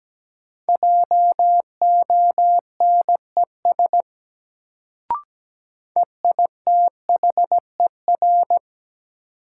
Morse code, heard as a keyed tone, about 17 wpm, 700 Hz. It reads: JONES EITHER
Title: Sets of 5 Words - New Word Drill - 17wpm